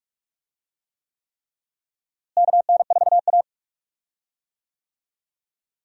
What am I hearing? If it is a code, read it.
Morse code, heard as a keyed tone, about 46 wpm, 700 Hz. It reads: KN4A
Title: USA Call Signs - 4 Characters - Repeated 3x - 45wpm